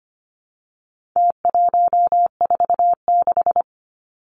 Morse code, heard as a keyed tone, about 25 wpm, 700 Hz. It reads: T146
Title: Numbers - Contest Style - 4-Digit 25wpm